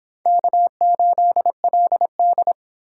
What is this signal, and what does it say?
Morse code, heard as a keyed tone, about 31 wpm, 700 Hz. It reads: K8LD